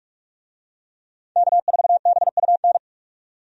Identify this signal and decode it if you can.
Morse code, heard as a keyed tone, about 46 wpm, 700 Hz. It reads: K4BUN
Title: USA Call Signs - 5 Characters - 45wpm